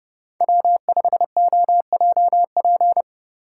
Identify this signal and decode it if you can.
Morse code, heard as a keyed tone, about 32 wpm, 700 Hz. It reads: W5OJP
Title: USA Call Signs - 5 Characters - 3x Character Spacing - 30wpm